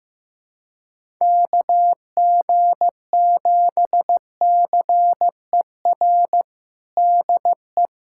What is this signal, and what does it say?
Morse code, heard as a keyed tone, about 15 wpm, 700 Hz. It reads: KG7CER DE
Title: USA Callsign DE Callsign 15wpm